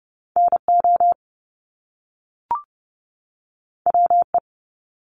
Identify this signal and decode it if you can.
Morse code, heard as a keyed tone, about 30 wpm, 700 Hz. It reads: NO WE
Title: Top Two Letter Combinations in Words 30wpm